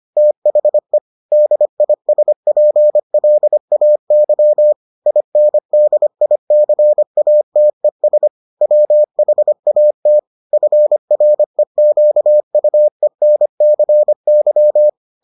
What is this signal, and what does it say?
Morse code, heard as a keyed tone, about 25 wpm, 600 Hz. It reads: THE DISPLAY INDICATES WHAT FREQUENCY